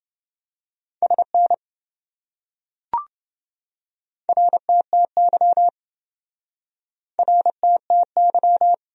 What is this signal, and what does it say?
Morse code, heard as a keyed tone, about 26 wpm, 700 Hz. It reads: SN RTTY RTTY